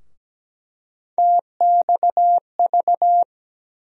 Morse code, heard as a keyed tone, about 17 wpm, 700 Hz. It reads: TXV
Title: USA Call Sign Postfixes - 3 Letters 17wpm